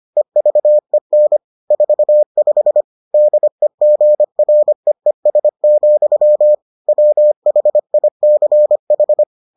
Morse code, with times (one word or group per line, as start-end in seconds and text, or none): 0.2-1.4 EVEN
1.7-2.8 45
3.1-6.5 DEGREES,
6.9-9.2 WHICH